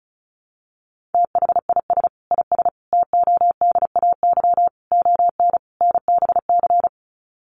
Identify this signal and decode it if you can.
Morse code, heard as a keyed tone, about 34 wpm, 700 Hz. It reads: THIS IS TODAY ON NBC
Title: Sets of 5 Words Encore 35wpm